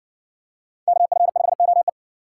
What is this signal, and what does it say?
Morse code, heard as a keyed tone, about 56 wpm, 700 Hz. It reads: KF5OE